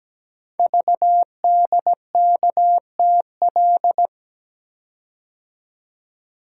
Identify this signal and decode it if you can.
Morse code, heard as a keyed tone, about 17 wpm, 700 Hz. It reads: VDKTL
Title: Code Groups - 5 Letters - 17wpm